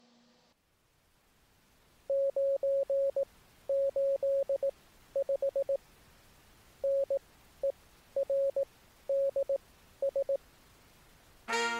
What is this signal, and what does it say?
Morse code, heard as a keyed tone, about 18 wpm, spaced as for 11 wpm, 550 Hz. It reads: 985NERDS